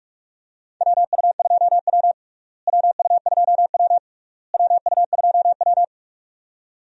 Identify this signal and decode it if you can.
Morse code, heard as a keyed tone, about 45 wpm, 700 Hz. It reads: WU2W WU2W WU2W
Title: USA Call Signs - 4 Characters - Repeated 3x - 45wpm